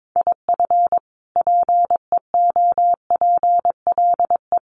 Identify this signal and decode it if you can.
Morse code, heard as a keyed tone, about 22 wpm, 700 Hz. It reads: IF PEOPLE
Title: Sentences from Top 1300 Words 22wpm